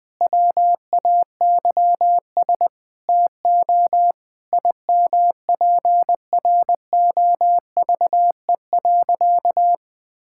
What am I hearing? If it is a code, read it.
Morse code, heard as a keyed tone, about 20 wpm, 700 Hz. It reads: WAYS TO IMPROVE.
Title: Sentences with 6-Word Phrases - 20wpm